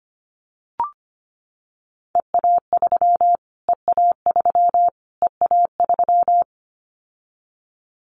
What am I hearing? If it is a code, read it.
Morse code, heard as a keyed tone, about 25 wpm, 700 Hz. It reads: EA3 EA3 EA3
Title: Callsign Prefixes for Top 25 DXCC - Repeated 3x - 25wpm